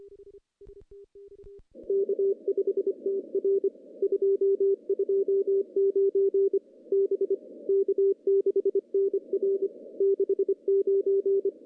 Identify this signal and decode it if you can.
Morse code, heard as a keyed tone, about 25 wpm, 410 Hz. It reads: K5TR 229 B K6NR 69
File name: contests / CW-SS-k5tr- -0100